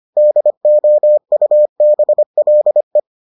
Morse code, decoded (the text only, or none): DOUBLE